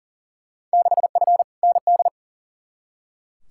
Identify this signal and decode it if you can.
Morse code, heard as a keyed tone, about 40 wpm, 700 Hz. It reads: 6F ND
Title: Field Day Exchanges Encore - 40wpm